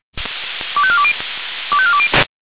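Five-tone selective call (ZVEI), heard as quick runs of five tones, tones of about 70 ms. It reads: 25420 25420
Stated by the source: ZVEI five-tone selective call received on SDR.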